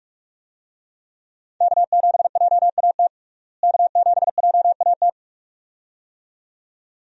Morse code, heard as a keyed tone, about 46 wpm, 700 Hz. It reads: K7JAT K7JAT
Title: USA Call Signs - 5 Characters - Repeated 2x - 45wpm